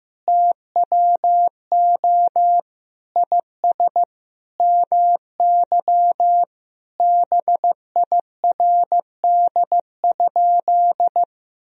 Morse code, heard as a keyed tone, about 15 wpm, 700 Hz. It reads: TWO IS MY BIRD?
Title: Sentences from Top 1000 Words - Review Part 2 - 15wpm